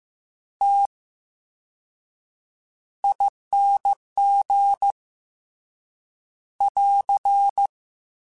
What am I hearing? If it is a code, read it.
Morse code, heard as a keyed tone, about 15 wpm, spaced as for 7 wpm, 800 Hz. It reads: TING <AR>
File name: Special INGs-mp3s / Special INGs-mp3s / VerySlowToFaster Ing @13